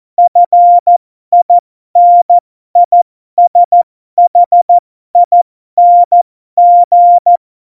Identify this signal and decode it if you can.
Morse code, heard as a keyed tone, about 14 wpm, spaced as for 12 wpm, 700 Hz. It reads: FINISHING
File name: Morsekurs / Witze